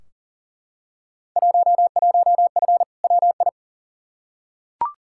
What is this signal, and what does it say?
Morse code, heard as a keyed tone, about 40 wpm, 700 Hz. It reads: 11F WI
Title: Field Day Exchanges Encore - 40wpm